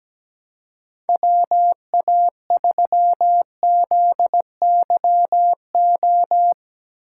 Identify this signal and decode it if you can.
Morse code, heard as a keyed tone, about 17 wpm, 700 Hz. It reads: WA3ZYO
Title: USA Call Signs - 6x Character Spacing - 17wpm